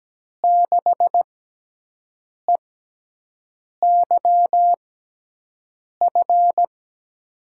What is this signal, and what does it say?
Morse code, heard as a keyed tone, about 17 wpm, spaced as for 6 wpm, 700 Hz. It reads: 6EYF
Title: USA Call Signs - 6 Characters - 6x Character Spacing - 17wpm